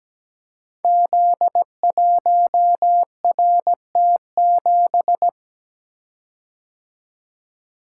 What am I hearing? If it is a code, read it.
Morse code, heard as a keyed tone, about 17 wpm, 700 Hz. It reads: Z1RT7
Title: Code Groups - 5 Letter:Numbers - 17wpm